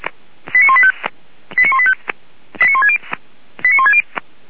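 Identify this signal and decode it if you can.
Five-tone selective call (ZVEI), heard as quick runs of five tones, tones of about 70 ms. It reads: 89185 89185 89160
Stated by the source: ZVEI five-tone selective call 89170 received with an SDR.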